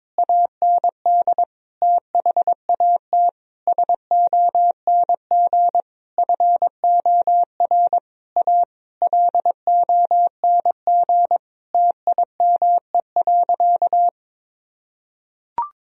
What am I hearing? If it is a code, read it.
Morse code, heard as a keyed tone, about 22 wpm, 700 Hz. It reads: AND THAT SONG FOR A LONG TIME.
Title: Sentences from Top 1000 Words - Review Part 2 - 22wpm